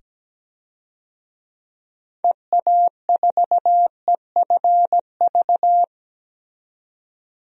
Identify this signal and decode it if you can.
Morse code, heard as a keyed tone, about 17 wpm, 700 Hz. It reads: EA4EFV
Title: DXCC-100 Call Signs - 6 Characters - 17wpm